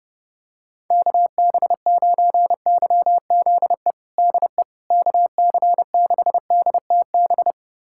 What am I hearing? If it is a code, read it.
Morse code, heard as a keyed tone, about 32 wpm, 700 Hz. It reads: KB9YZE DE KC6DTB
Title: USA Callsign DE Callsign - 6 Characters - 30wpm